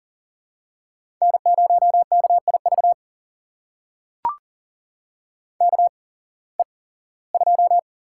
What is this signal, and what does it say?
Morse code, heard as a keyed tone, about 40 wpm, spaced as for 21 wpm, 700 Hz. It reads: N0KIV KE2
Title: USA Call Signs - 8x Character Spacing - 40wpm